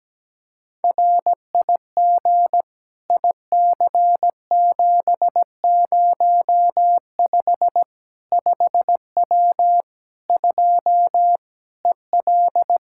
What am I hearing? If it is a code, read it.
Morse code, heard as a keyed tone, about 17 wpm, 700 Hz. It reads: RIG IC705 5W 2 EL
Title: K1USN SST QSO Practice 17wpm